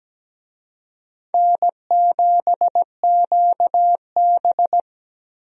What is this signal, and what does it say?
Morse code, heard as a keyed tone, about 17 wpm, 700 Hz. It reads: N7QB
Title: USA Call Signs - 4 Characters - Repeated 3x - 17wpm